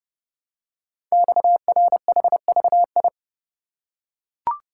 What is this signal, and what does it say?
Morse code, heard as a keyed tone, about 30 wpm, 690 Hz. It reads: XRHVI
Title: Code Groups - 5 Letters - 30wpm